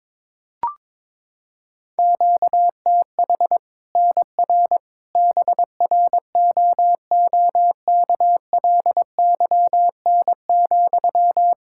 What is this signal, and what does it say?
Morse code, heard as a keyed tone, about 22 wpm, 700 Hz. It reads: QTH NR BROOKLYN,
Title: Practice QSO - Exchange 1 - Beginning - 22wpm